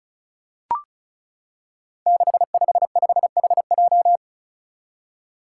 Morse code, heard as a keyed tone, about 36 wpm, 700 Hz. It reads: 655HJ
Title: Five Challenging Letter-Number Combinations 35wpm